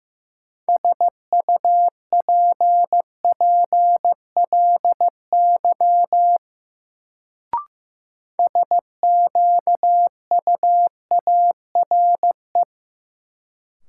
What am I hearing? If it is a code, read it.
Morse code, heard as a keyed tone, about 15 wpm, 700 Hz. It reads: SUPPLY SQUARE
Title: Sentences from Top 1200 Words - New Word Drill Alt - 15wpm